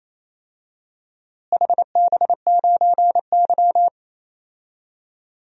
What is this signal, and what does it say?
Morse code, heard as a keyed tone, about 27 wpm, 700 Hz. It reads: HB9Y
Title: DXCC-100 Call Signs - 4 Characters - Repeated 3x - 28wpm